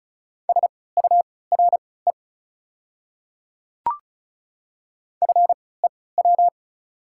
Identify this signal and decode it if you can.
Morse code, heard as a keyed tone, about 34 wpm, spaced as for 20 wpm, 690 Hz. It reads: SURE FEW
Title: 3x Character Spacing - Top 300 Words - 35wpm